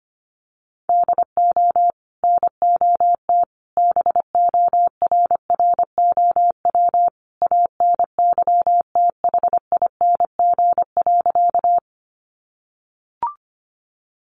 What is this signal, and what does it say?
Morse code, heard as a keyed tone, about 25 wpm, 700 Hz. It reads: DO NOT BORROW ANYTHING.
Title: Sentences from Top 1500 Words 25wpm